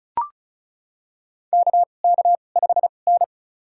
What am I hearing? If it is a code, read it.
Morse code, heard as a keyed tone, about 36 wpm, spaced as for 25 wpm, 700 Hz. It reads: KK5N
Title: USA Call Signs - 2x Character Spacing - 35wpm